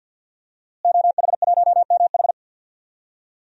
Morse code, heard as a keyed tone, about 49 wpm, 700 Hz. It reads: OH1MH
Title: DXCC-100 Call Signs - 5 Characters - Repeated 2x - 50wpm